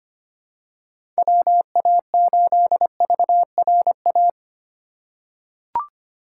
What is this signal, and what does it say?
Morse code, heard as a keyed tone, about 25 wpm, 700 Hz. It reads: WA8VRA
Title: USA Call Signs - 6 Characters - 4x Character Spacing - 25wpm